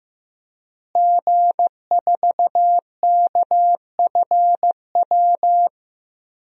Morse code, heard as a keyed tone, about 15 wpm, 700 Hz. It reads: G4KFW
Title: DXCC-100 Call Signs - 5 Characters - Repeated 2x - 15wpm